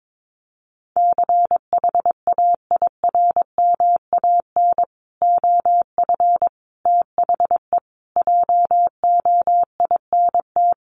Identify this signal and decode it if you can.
Morse code, heard as a keyed tone, about 22 wpm, 700 Hz. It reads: CHAIRMAN OF THE JOINT
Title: Sets of 5 Words Encore 22wpm